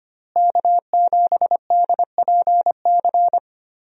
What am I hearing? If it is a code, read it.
Morse code, heard as a keyed tone, about 25 wpm, 700 Hz. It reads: K7DPC
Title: USA Call Signs - 5 Characters - Repeated 2x - 25wpm